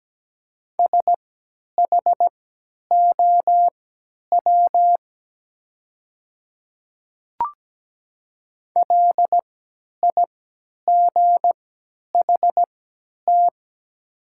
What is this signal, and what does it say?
Morse code, heard as a keyed tone, about 17 wpm, spaced as for 10 wpm, 700 Hz. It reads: SHOW LIGHT
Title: 3x Character Spacing - Top 300 Words - 17wpm